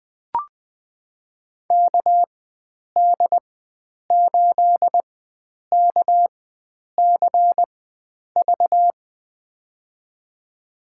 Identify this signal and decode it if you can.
Morse code, heard as a keyed tone, about 21 wpm, spaced as for 9 wpm, 700 Hz. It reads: KD8KCV
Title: USA Call Signs - 4x Character Spacing - 20wpm